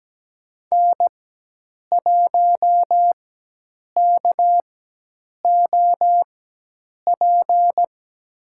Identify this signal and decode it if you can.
Morse code, heard as a keyed tone, about 17 wpm, spaced as for 8 wpm, 700 Hz. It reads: N1KOP
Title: USA Call Signs - 5 Characters - 4x Character Spacing - 17wpm